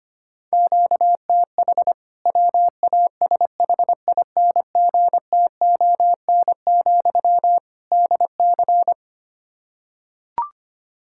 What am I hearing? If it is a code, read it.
Morse code, heard as a keyed tone, about 25 wpm, 700 Hz. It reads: QTH WASHINGTON, DC
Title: K1USN SST QSO Practice 25wpm